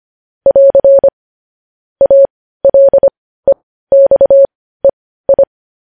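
Morse code, heard as a keyed tone, about 25 wpm, spaced as for 15 wpm, 550 Hz. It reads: <AR> ALEXEI
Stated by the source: Morse code practice of news headlines at 15 wpm.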